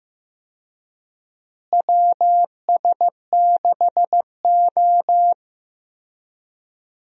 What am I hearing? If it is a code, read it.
Morse code, heard as a keyed tone, about 16 wpm, 700 Hz. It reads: WS6O
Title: USA Call Signs - 4 Characters - Repeated 3x - 15wpm